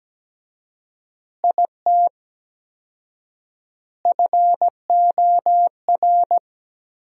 Morse code, heard as a keyed tone, about 17 wpm, 700 Hz. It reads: IT FOR
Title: 4x Word-Spacing - Sentences from Top 200 Words - 17wpm